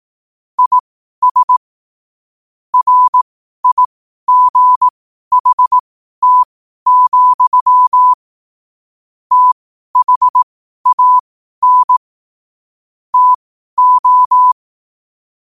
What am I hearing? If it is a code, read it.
Morse code, heard as a keyed tone, about 18 wpm, spaced as for 12 wpm, 1000 Hz. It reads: IS RIGHT, THAN TO
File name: qotdfiles / 12wpm